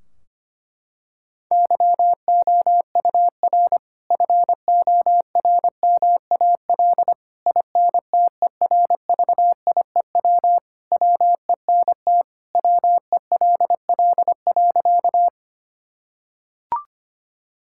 Morse code, heard as a keyed tone, about 25 wpm, 700 Hz. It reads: YOUR FORMAL INTERVIEW WENT WELL.
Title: Sentences from Top 1600 Words 25wpm